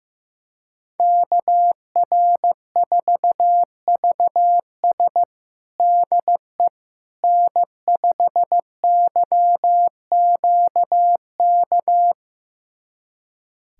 Morse code, heard as a keyed tone, about 16 wpm, 700 Hz. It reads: KR4VS DE N5YQK
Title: USA Callsign DE Callsign - 5 Characters - 15wpm